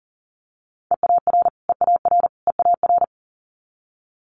Morse code, heard as a keyed tone, about 37 wpm, 700 Hz. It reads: EAR EAR EAR
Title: Multi-letter Phonemes Repeated 3x 40wpm